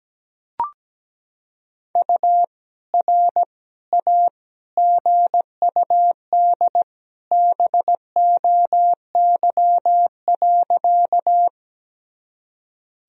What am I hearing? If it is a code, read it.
Morse code, heard as a keyed tone, about 17 wpm, 700 Hz. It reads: U R A GUD BOY.
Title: Sentences from Top 200 Words and Abbreviations 17wpm